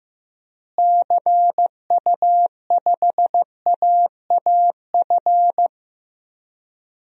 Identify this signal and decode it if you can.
Morse code, heard as a keyed tone, about 15 wpm, 700 Hz. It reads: CU5AAF
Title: DXCC-100 Call Signs - 6 Characters - Repeated 3x - 15wpm